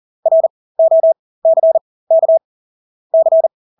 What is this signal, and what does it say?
Morse code, heard as a keyed tone, about 40 wpm, spaced as for 20 wpm, 650 Hz. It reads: ROCK C
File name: Engelske ord 40-20 wpm